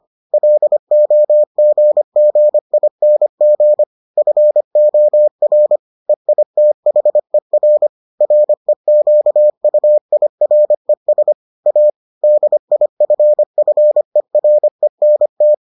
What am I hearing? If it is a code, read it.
Morse code, heard as a keyed tone, about 25 wpm, 600 Hz. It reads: LOGGING FOR EITHER REQUIRES A DIFFERENT